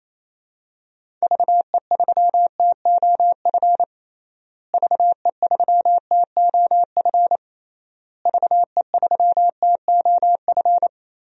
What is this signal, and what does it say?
Morse code, heard as a keyed tone, about 28 wpm, 700 Hz. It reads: VE3TOF VE3TOF VE3TOF
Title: DXCC-100 Call Signs - 6 Characters - Repeated 3x - 28wpm